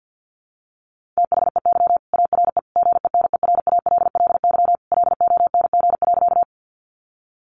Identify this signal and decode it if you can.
Morse code, heard as a keyed tone, about 50 wpm, 700 Hz. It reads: THEY ARE GENERALLY LONG.
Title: Sentences from Top 600 Words 50wpm